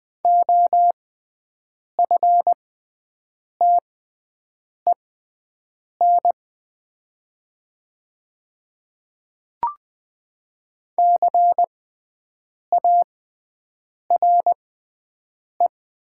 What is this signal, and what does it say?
Morse code, heard as a keyed tone, about 20 wpm, spaced as for 7 wpm, 700 Hz. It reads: OFTEN CARE
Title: 6x Character Spacing - Top 300 Words - 20wpm